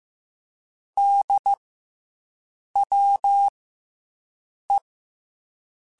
Morse code, heard as a keyed tone, about 15 wpm, spaced as for 6 wpm, 800 Hz. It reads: DWE